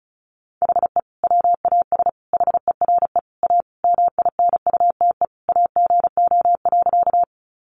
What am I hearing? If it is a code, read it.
Morse code, heard as a keyed tone, about 34 wpm, 700 Hz. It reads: HE WAS HERE A MINUTE AGO.
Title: Sentences from Top 500 Words 35wpm